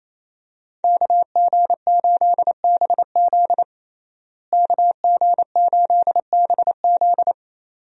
Morse code, heard as a keyed tone, about 27 wpm, 700 Hz. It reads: KG8BZ KG8BZ
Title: USA Call Signs - 5 Characters - Repeated 2x - 28wpm